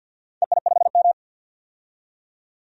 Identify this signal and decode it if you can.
Morse code, heard as a keyed tone, about 47 wpm, 700 Hz. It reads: EI5M
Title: DXCC-100 Call Signs - 4 Characters - 50wpm